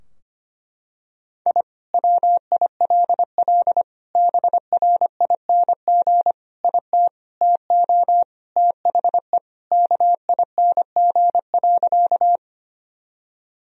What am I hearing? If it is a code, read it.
Morse code, heard as a keyed tone, about 25 wpm, 700 Hz. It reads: I WILL BRING IT TO THE KING.
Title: Sentences from Top 1000 Words - Review Part 3 - 25wpm